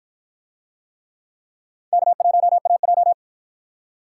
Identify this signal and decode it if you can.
Morse code, heard as a keyed tone, about 51 wpm, 700 Hz. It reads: K1AJ